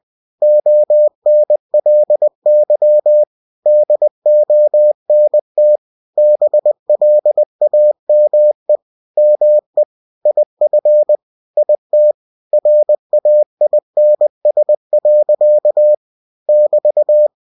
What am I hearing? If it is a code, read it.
Morse code, heard as a keyed tone, about 20 wpm, 600 Hz. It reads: ONLY DONT BLAME ME IF IT RAINS. <BT>
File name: cw-books / winnie-the-pooh / winnie-20wpm-page-0105